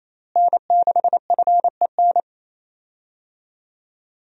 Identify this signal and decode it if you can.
Morse code, heard as a keyed tone, about 27 wpm, 700 Hz. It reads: N6FEN
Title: USA Call Signs - 5 Characters - Repeated 3x - 28wpm